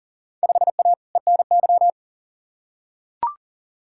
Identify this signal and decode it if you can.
Morse code, heard as a keyed tone, about 40 wpm, 700 Hz. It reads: 5A ENY
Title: Field Day Exchanges - 40wpm